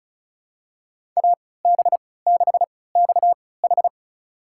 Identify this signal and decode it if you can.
Morse code, heard as a keyed tone, about 36 wpm, spaced as for 20 wpm, 690 Hz. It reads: AB6XH
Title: USA Call Signs - 5 Characters - 3x Character Spacing - 35wpm